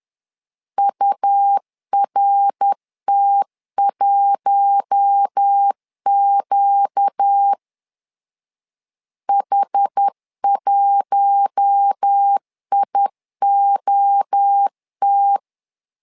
Morse code, heard as a keyed tone, about 10 wpm, 800 Hz. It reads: URT1Q H1IOT
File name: LEZIONI / 4 Q1 10wpm 2448